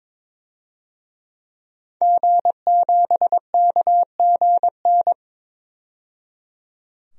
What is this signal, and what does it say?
Morse code, heard as a keyed tone, about 22 wpm, 700 Hz. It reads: G7KGN
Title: DXCC-100 Call Signs - 5 Characters - 22wpm